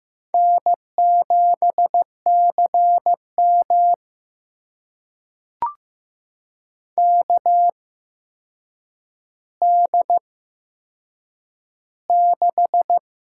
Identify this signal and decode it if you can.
Morse code, heard as a keyed tone, about 15 wpm, spaced as for 7 wpm, 700 Hz. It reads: N7CM KD6
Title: USA Call Signs - 4 Characters - 8x Character Spacing - 15wpm